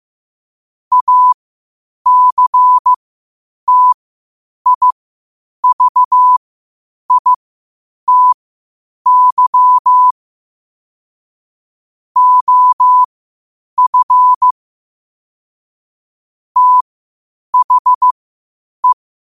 Morse code, heard as a keyed tone, about 15 wpm, spaced as for 8 wpm, 1000 Hz. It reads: ACTIVITY OF THE
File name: qotdfiles / 8wpm